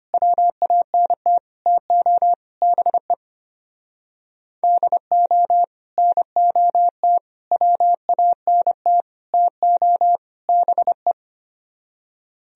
Morse code, heard as a keyed tone, about 27 wpm, 700 Hz. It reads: WANT TO BE DO NOT WANT TO BE
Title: Speed-Racing - Sets of 5 Words 25wpm